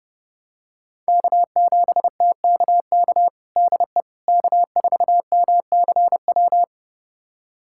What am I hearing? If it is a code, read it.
Morse code, heard as a keyed tone, about 30 wpm, 700 Hz. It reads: K7TKK DE K4MCW
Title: USA Callsign DE Callsign - 5 Characters - 30wpm